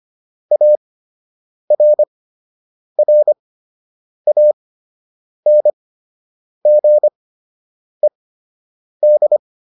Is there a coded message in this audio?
Yes: Morse code from 0.5 to 9.4 s, heard as a keyed tone, about 25 wpm, spaced as for 8 wpm, 600 Hz. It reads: ARRANGED